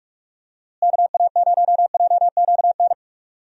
Morse code, heard as a keyed tone, about 41 wpm, 700 Hz. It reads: KA0JQN
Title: USA Call Signs - 6 Characters - Repeated 3x - 45wpm